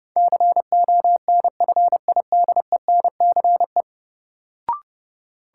Morse code, heard as a keyed tone, about 30 wpm, 700 Hz. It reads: CONFIDENCE
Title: Sentences from Top 1500 Words - New Word Drill - 30wpm